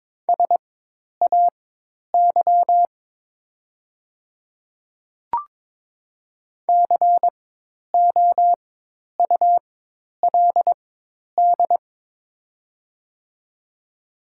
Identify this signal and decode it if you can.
Morse code, heard as a keyed tone, about 22 wpm, spaced as for 10 wpm, 700 Hz. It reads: SAY COULD
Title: 4x Character Spacing - Top 300 Words - 22wpm